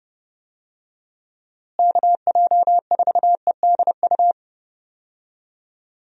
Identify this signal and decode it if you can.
Morse code, heard as a keyed tone, about 31 wpm, 700 Hz. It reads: KJ4EDU
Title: USA Call Signs - 6 Characters - Repeated 3x - 30wpm